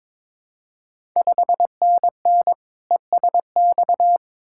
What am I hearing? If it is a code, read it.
Morse code, heard as a keyed tone, about 22 wpm, 700 Hz. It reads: 5NN ESX
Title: Ontario QSO Party Exchange 22wpm